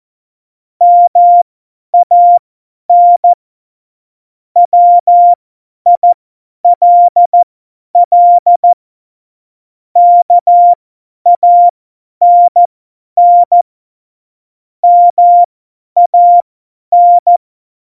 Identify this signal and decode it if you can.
Morse code, heard as a keyed tone, about 14 wpm, spaced as for 10 wpm, 700 Hz. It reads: MAN WILL KANN MAN
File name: Morsekurs / Witze / Witz 31. Oktober 17, Tempo 50